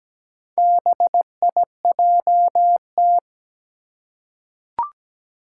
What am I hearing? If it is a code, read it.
Morse code, heard as a keyed tone, about 17 wpm, 700 Hz. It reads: BIJT
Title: Four Challenging Letter-Number Combinations 17wpm